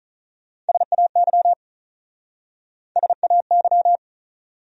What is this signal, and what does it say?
Morse code, heard as a keyed tone, about 43 wpm, 700 Hz. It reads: SAY SAY
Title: Speed-Racing Top 500 Words 35wpm